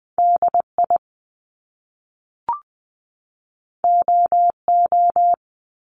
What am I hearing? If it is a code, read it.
Morse code, heard as a keyed tone, about 20 wpm, 700 Hz. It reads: DI OO
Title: Top Two Letter Combinations in Words 20wpm